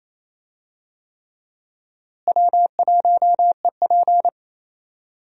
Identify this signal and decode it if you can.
Morse code, heard as a keyed tone, about 28 wpm, 700 Hz. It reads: W1EP
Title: USA Call Signs - 4 Characters - Repeated 3x - 28wpm